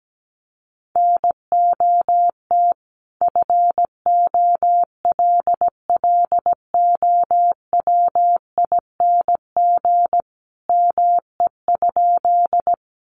Morse code, heard as a keyed tone, about 17 wpm, 700 Hz. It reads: NOT FOLLOWING ME?
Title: Sentences from Top 700 Words 17wpm